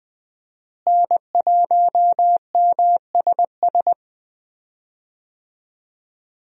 Morse code, heard as a keyed tone, about 21 wpm, 700 Hz. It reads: N1MSS
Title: USA Call Signs - 5 Characters - Repeated 2x - 20wpm